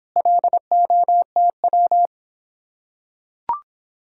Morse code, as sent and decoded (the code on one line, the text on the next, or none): .-.. --- - .--
LOTW